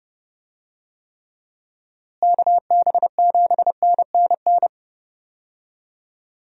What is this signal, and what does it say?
Morse code, heard as a keyed tone, about 30 wpm, 700 Hz. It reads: KB7NNN
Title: USA Call Signs - 6 Characters - Repeated 2x - 30wpm